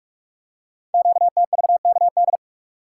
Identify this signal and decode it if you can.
Morse code, heard as a keyed tone, about 44 wpm, 700 Hz. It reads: QTVKD
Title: Code Groups - 5 Letter:Numbers - 45wpm